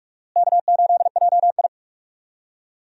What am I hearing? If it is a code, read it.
Morse code, heard as a keyed tone, about 41 wpm, 700 Hz. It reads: K8JI
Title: USA Call Signs - 4 Characters - Repeated 3x - 45wpm